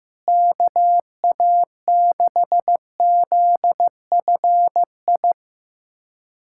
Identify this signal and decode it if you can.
Morse code, heard as a keyed tone, about 16 wpm, 700 Hz. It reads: KA6ZFI
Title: USA Callsign DE Callsign - 6 Characters - 15wpm